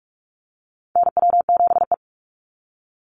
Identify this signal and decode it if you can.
Morse code, heard as a keyed tone, about 44 wpm, 700 Hz. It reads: NW7E
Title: NA SOTA Chaser Callsigns 45wpm